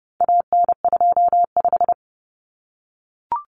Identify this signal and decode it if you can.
Morse code, heard as a keyed tone, about 30 wpm, 700 Hz. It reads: AN25
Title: Common Cut Numbers - 4-Digit 30wpm